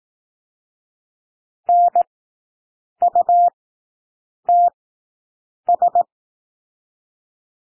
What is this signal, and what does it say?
Morse code, heard as a keyed tone, about 18 wpm, spaced as for 7 wpm, 700 Hz. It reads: NUTS
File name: SOUND WORDS 18 wpm